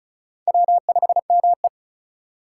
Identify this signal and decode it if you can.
Morse code, heard as a keyed tone, about 34 wpm, 700 Hz. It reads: W5ME
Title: USA Call Signs - 4 Characters - 6x Character Spacing - 35wpm